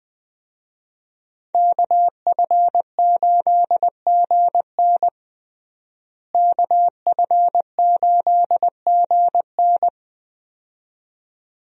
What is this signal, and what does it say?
Morse code, heard as a keyed tone, about 20 wpm, 700 Hz. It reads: KF8GN KF8GN
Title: USA Call Signs - 5 Characters - Repeated 2x - 20wpm